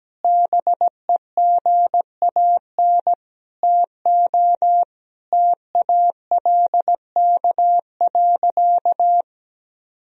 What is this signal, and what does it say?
Morse code, heard as a keyed tone, about 17 wpm, 700 Hz. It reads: BEGAN TO TALK.
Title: Sentences from Top 400 Words and Abbreviations 17wpm